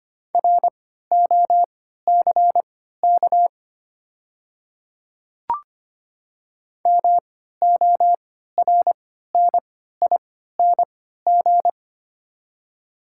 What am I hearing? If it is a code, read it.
Morse code, heard as a keyed tone, about 25 wpm, spaced as for 14 wpm, 700 Hz. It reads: ROCK MORNING
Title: 3x Character Spacing - Top 500 Words - 25wpm